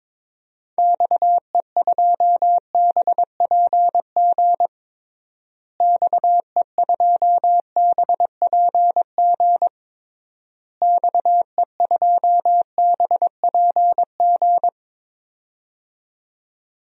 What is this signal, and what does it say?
Morse code, heard as a keyed tone, about 21 wpm, 700 Hz. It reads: XE2BPG XE2BPG XE2BPG
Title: DXCC-100 Call Signs - 6 Characters - Repeated 3x - 22wpm